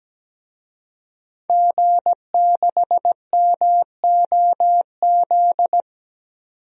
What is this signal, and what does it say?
Morse code, heard as a keyed tone, about 17 wpm, 700 Hz. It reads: G6MOZ
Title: DXCC-100 Call Signs - 5 Characters - 17wpm